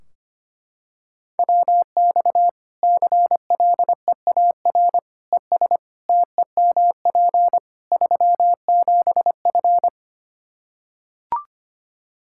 Morse code, heard as a keyed tone, about 25 wpm, 700 Hz. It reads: WX CLEAR ES TEMP 37F
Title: K1USN SST QSO Practice 25wpm